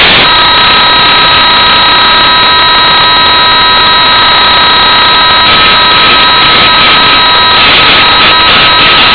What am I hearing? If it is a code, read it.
PSK31, about 31.25 baud.